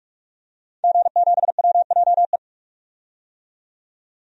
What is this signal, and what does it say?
Morse code, heard as a keyed tone, about 46 wpm, 700 Hz. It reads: G7WJE
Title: DXCC-100 Call Signs - 5 Characters - 45wpm